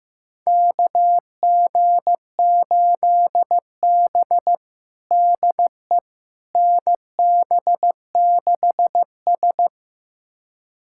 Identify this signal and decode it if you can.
Morse code, heard as a keyed tone, about 15 wpm, 700 Hz. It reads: KG8B DE NB6S
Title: USA Callsign DE Callsign - 4 Characters - 15wpm